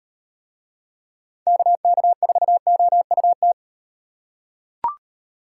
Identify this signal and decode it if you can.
Morse code, heard as a keyed tone, about 38 wpm, 700 Hz. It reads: KK4OUT